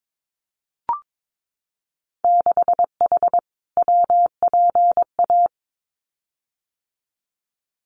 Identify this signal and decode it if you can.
Morse code, heard as a keyed tone, about 22 wpm, 700 Hz. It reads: 6H WPA
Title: Winter Field Day Exchanges Encore 22wpm